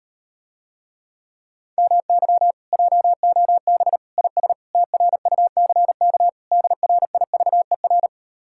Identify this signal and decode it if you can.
Morse code, heard as a keyed tone, about 39 wpm, 700 Hz. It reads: MY JOB IS TRUCK DRIVER